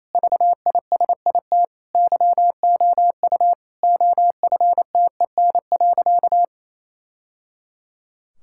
Morse code, heard as a keyed tone, about 28 wpm, 700 Hz. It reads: VISIT YOU OFTEN.
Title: Sentences from Top 1000 Words - Review Part 6 - 28wpm